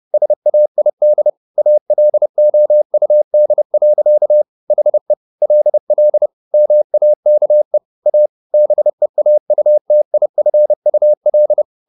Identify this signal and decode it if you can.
Morse code, heard as a keyed tone, about 30 wpm, 600 Hz. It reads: SAID ALOUD. HE LL MAKE A BEAUTIFUL
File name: the old man and the sea / the old man and the sea 1 30wpm